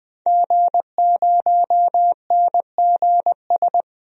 Morse code, heard as a keyed tone, about 20 wpm, 700 Hz. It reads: G0NGS
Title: DXCC-100 Call Signs - 5 Characters - 20wpm